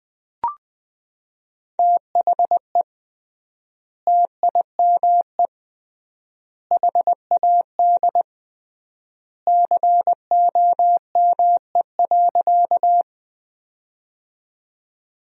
Morse code, heard as a keyed tone, about 20 wpm, 700 Hz. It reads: THE TIME HAD COME.
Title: 3x Word-Spacing - Sentences from Top 100 Words - 20wpm